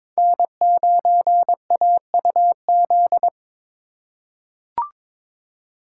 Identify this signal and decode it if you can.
Morse code, heard as a keyed tone, about 22 wpm, 700 Hz. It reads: N9AUZ
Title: USA Call Signs - 5 Characters - 8x Character Spacing - 22wpm